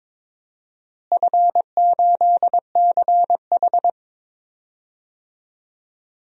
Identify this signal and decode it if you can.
Morse code, heard as a keyed tone, about 22 wpm, 700 Hz. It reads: F8CH
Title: DXCC-100 Call Signs - 4 Characters - Repeated 2x - 22wpm